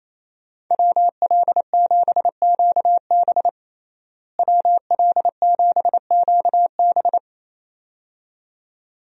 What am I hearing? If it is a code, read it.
Morse code, heard as a keyed tone, about 27 wpm, 700 Hz. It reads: WL7QB WL7QB
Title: USA Call Signs - 5 Characters - Repeated 2x - 28wpm